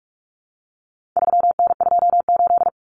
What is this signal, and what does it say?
Morse code, heard as a keyed tone, about 44 wpm, 700 Hz. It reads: 3N28
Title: Common Cut Numbers - 4-Digit 45wpm